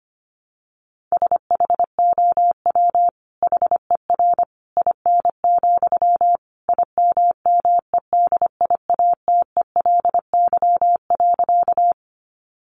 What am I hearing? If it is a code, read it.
Morse code, heard as a keyed tone, about 25 wpm, 700 Hz. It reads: SHOW HER IN, IMMEDIATELY.
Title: Sentences from Top 1100 Words 25wpm